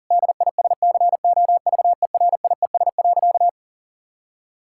Morse code, heard as a keyed tone, about 40 wpm, 700 Hz. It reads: DISCOVERIES.
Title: Pangram Challenge v2 - 40wpm